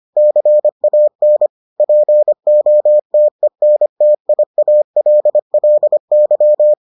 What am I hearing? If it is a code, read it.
Morse code, heard as a keyed tone, about 25 wpm, 600 Hz. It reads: CAN POTENTIALLY